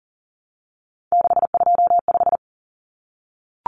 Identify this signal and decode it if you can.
Morse code, heard as a keyed tone, about 40 wpm, 700 Hz. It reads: 625
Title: Common Cut Numbers - 3-Digit 40wpm